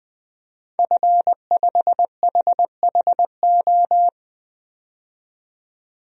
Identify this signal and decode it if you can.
Morse code, heard as a keyed tone, about 20 wpm, 700 Hz. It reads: F5HHO
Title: DXCC-100 Call Signs - 5 Characters - 20wpm